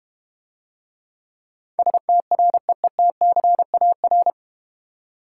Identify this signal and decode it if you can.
Morse code, heard as a keyed tone, about 32 wpm, 700 Hz. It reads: STREETCAR